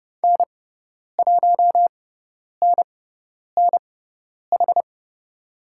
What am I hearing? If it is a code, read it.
Morse code, heard as a keyed tone, about 30 wpm, spaced as for 10 wpm, 700 Hz. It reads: N1NNH
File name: Lesson 18g - Letter m - Call Signs 30wpm